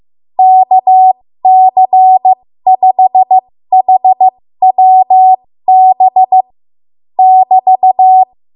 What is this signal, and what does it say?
Morse code, heard as a keyed tone, about 15 wpm, spaced as for 13 wpm, 750 Hz. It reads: KC5HWB <BT>
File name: Morse / 13WPM